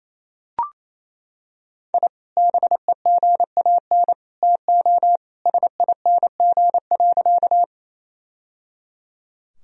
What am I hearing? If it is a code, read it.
Morse code, heard as a keyed tone, about 28 wpm, 700 Hz. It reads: I BEGAN TO SING.
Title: Sentences from Top 1000 Words - Review Part 2 - 28wpm